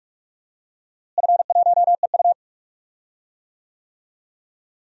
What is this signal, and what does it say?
Morse code, heard as a keyed tone, about 46 wpm, 700 Hz. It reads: F1EU
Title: DXCC-100 Call Signs - 4 Characters - Repeated 2x - 45wpm